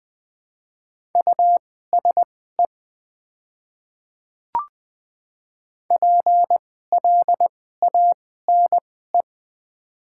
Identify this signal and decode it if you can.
Morse code, heard as a keyed tone, about 20 wpm, spaced as for 14 wpm, 700 Hz. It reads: USE PLANE
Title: 2x Character Spacing - Top 500 Words - 20wpm